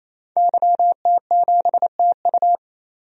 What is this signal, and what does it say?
Morse code, heard as a keyed tone, about 28 wpm, 700 Hz. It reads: YT7TU
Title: DXCC-100 Call Signs - 5 Characters - 28wpm